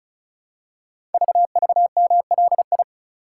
Morse code, heard as a keyed tone, about 36 wpm, 690 Hz. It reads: VVMLI
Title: Code Groups - 5 Letters - 35wpm